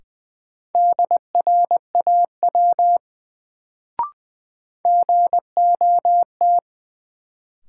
Morse code, heard as a keyed tone, about 20 wpm, 700 Hz. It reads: DRAW GOT
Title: Sentences from Top 300 Words - New Word Drill - 20wpm